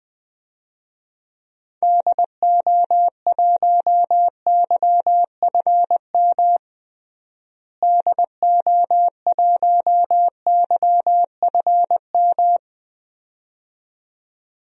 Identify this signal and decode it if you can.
Morse code, heard as a keyed tone, about 20 wpm, 700 Hz. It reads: DO1YFM DO1YFM
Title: DXCC-100 Call Signs - 6 Characters - Repeated 2x - 20wpm